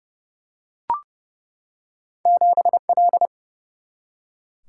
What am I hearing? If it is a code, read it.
Morse code, heard as a keyed tone, about 30 wpm, 700 Hz. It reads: 7 L
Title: Single-Letter Single-Number 30wpm